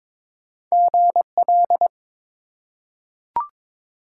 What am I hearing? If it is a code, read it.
Morse code, heard as a keyed tone, about 21 wpm, 700 Hz. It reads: GL
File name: Top 90 QSO Elements - 3x Character Spacing - 22wpm